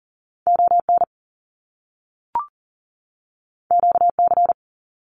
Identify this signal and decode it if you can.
Morse code, heard as a keyed tone, about 40 wpm, 700 Hz. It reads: ON QC
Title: Canadian Provinces and Territories 40wpm